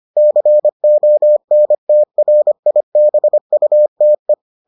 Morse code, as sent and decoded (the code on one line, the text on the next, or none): -.-. --- -. - .-. .. -... ..- - .
CONTRIBUTE